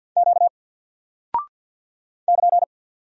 Morse code, heard as a keyed tone, about 48 wpm, 700 Hz. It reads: Q <KN>
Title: Single Character (Letter:Number:Punctuation:Prosign) 50wpm